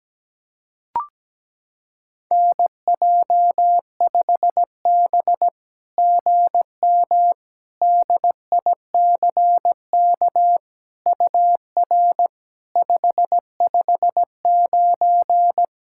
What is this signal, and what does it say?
Morse code, heard as a keyed tone, about 17 wpm, 700 Hz. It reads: NJ5B GM DICK UR 559
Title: SOTA QSO Practice 17wpm